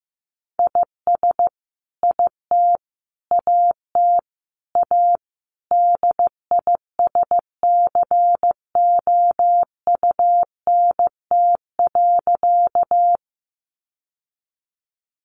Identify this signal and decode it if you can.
Morse code, heard as a keyed tone, about 15 wpm, 700 Hz. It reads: IS IT AT A DISCOUNT.
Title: Sentences from Top 1200 Words 15wpm